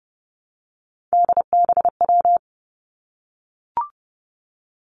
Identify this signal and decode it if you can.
Morse code, heard as a keyed tone, about 30 wpm, 700 Hz. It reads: DBW
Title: Three Challenging Letter-Number Combinations 30wpm